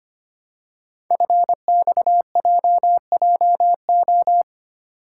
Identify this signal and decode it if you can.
Morse code, heard as a keyed tone, about 25 wpm, 700 Hz. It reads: FXJJO
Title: Code Groups - 5 Letters - 25wpm